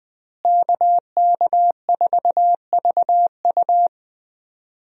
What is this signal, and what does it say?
Morse code, heard as a keyed tone, about 20 wpm, 700 Hz. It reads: KK4VU
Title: USA Call Signs - 5 Characters - Repeated 2x - 20wpm